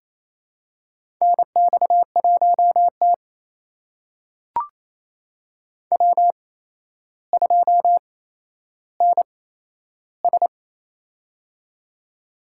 Morse code, heard as a keyed tone, about 28 wpm, spaced as for 12 wpm, 700 Hz. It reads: NX1T W2NS
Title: USA Call Signs - 4 Characters - 8x Character Spacing - 28wpm